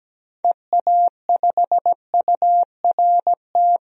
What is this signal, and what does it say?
Morse code, heard as a keyed tone, about 17 wpm, 700 Hz. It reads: EA5URT
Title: DXCC-100 Call Signs - 6 Characters - 17wpm